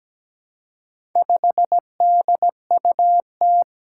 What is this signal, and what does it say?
Morse code, heard as a keyed tone, about 17 wpm, 700 Hz. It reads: 5DUT
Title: Four Challenging Letter-Number Combinations 17wpm